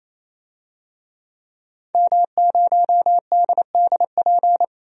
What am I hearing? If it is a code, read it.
Morse code, heard as a keyed tone, about 27 wpm, 700 Hz. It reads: M0DDP
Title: DXCC-100 Call Signs - 5 Characters - 28wpm